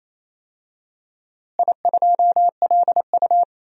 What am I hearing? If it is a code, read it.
Morse code, heard as a keyed tone, about 28 wpm, 700 Hz. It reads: I2LU
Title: DXCC-100 Call Signs - 4 Characters - Repeated 2x - 28wpm